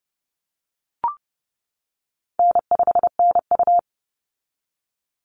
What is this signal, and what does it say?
Morse code, heard as a keyed tone, about 30 wpm, 700 Hz. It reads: N5NU
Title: Easy Call Signs - 30wpm